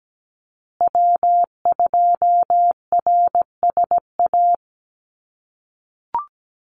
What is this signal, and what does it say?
Morse code, heard as a keyed tone, about 17 wpm, 700 Hz. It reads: W2RSA
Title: USA Call Signs Repeated 3x 17wpm